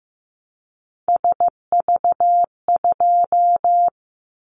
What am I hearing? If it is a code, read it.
Morse code, heard as a keyed tone, about 15 wpm, 700 Hz. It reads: SV2
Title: Callsign Prefixes for Top 50 DXCC - 15wpm